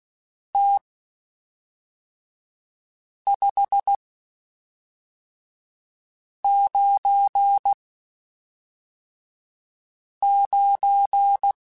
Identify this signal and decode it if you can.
Morse code, heard as a keyed tone, about 16 wpm, spaced as for 3 wpm, 800 Hz. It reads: T 599
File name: (spaced) / QSOs Spaced-mp3s / WA7QOX 1 @5 sp